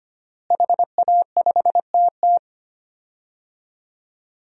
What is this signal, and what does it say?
Morse code, heard as a keyed tone, about 25 wpm, 700 Hz. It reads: HA5TT
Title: DXCC-100 Call Signs - 5 Characters - Repeated 3x - 25wpm